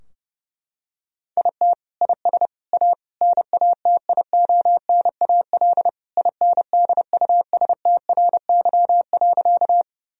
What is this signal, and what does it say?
Morse code, heard as a keyed tone, about 31 wpm, 700 Hz. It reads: IT IS A NATIONAL INDUSTRY.
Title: Sentences from Top 1000 Words - Review Part 4 - 30wpm